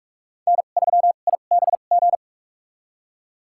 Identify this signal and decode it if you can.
Morse code, heard as a keyed tone, about 47 wpm, spaced as for 33 wpm, 700 Hz. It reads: N3IBG
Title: USA Call Signs - 5 Characters - 2x Character Spacing - 45wpm